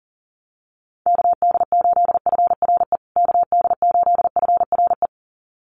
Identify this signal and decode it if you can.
Morse code, heard as a keyed tone, about 40 wpm, 700 Hz. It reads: KD8FRE KD8FRE
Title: USA Call Signs Repeated 2x 40wpm